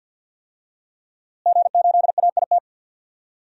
Morse code, heard as a keyed tone, about 51 wpm, 700 Hz. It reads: G8AIT
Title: DXCC-100 Call Signs - 5 Characters - Repeated 3x - 50wpm